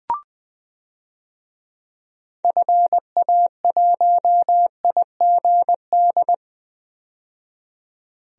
Morse code, heard as a keyed tone, about 20 wpm, 700 Hz. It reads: FA1IGD
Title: DXCC-100 Call Signs - 6 Characters - 20wpm